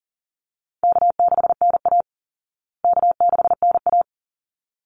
Morse code, heard as a keyed tone, about 40 wpm, 700 Hz. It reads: K6NA K6NA
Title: Easy Call Signs Repeated 2x - 40wpm